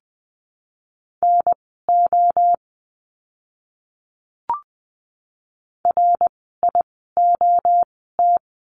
Morse code, heard as a keyed tone, about 20 wpm, spaced as for 15 wpm, 700 Hz. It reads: NO RIOT
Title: 061 Lesson 11f - Letter h - Words 20wpm